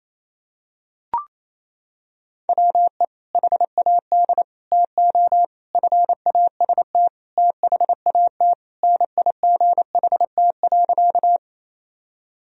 Morse code, heard as a keyed tone, about 28 wpm, 700 Hz. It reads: WE HAD TO FAST THAT NIGHT.
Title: Sentences from Top 1000 Words - Review Part 2 - 28wpm